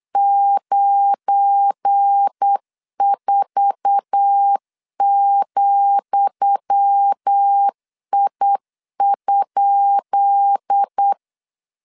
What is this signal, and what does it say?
Morse code, heard as a keyed tone, about 8 wpm, 800 Hz. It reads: 94,I?
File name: LEZIONI / 10 SIGNS / 10D COMA 8wpm 3749